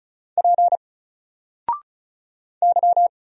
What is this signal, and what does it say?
Morse code, heard as a keyed tone, about 35 wpm, 700 Hz. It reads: P Y
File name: Single-Letter-Number-35wpm